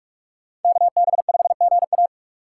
Morse code, heard as a keyed tone, about 42 wpm, 700 Hz. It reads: KB5GA